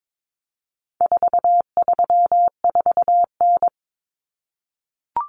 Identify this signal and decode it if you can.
Morse code, heard as a keyed tone, about 22 wpm, 700 Hz. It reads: 434N
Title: Common Cut Numbers - 4-Digit 22wpm